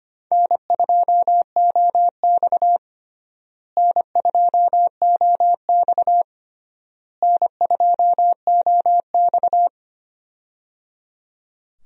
Morse code, heard as a keyed tone, about 25 wpm, 700 Hz. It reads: N2OX N2OX N2OX
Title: USA Call Signs - 4 Characters - Repeated 3x - 25wpm